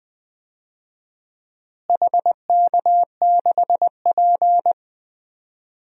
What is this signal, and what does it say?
Morse code, heard as a keyed tone, about 20 wpm, 700 Hz. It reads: HK6P
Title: DXCC-100 Call Signs - 4 Characters - Repeated 3x - 20wpm